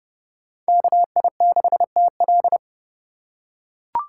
Morse code, heard as a keyed tone, about 32 wpm, 700 Hz. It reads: KI6TL
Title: USA Call Signs - 2x Character Spacing - 30wpm